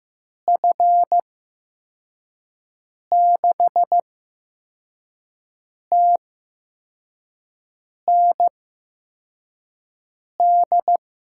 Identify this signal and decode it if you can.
Morse code, heard as a keyed tone, about 15 wpm, spaced as for 4 wpm, 700 Hz. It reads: F6TND
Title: USA Call Signs - 6 Characters - 8x Character Spacing - 15wpm